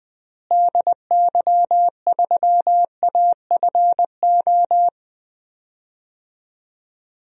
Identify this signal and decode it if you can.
Morse code, heard as a keyed tone, about 20 wpm, 700 Hz. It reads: DY3AFO
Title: DXCC-100 Call Signs - 6 Characters - Repeated 2x - 20wpm